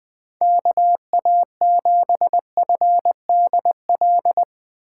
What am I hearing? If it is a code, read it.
Morse code, heard as a keyed tone, about 20 wpm, 700 Hz. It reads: KA7FDL